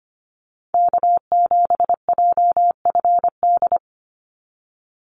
Morse code, heard as a keyed tone, about 25 wpm, 690 Hz. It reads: K7JFD
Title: NA SOTA Activator Callsigns 25wpm